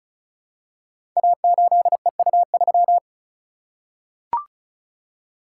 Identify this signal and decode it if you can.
Morse code, heard as a keyed tone, about 36 wpm, 700 Hz. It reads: A8EU3
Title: Code Groups - 5 Letter:Numbers - 35wpm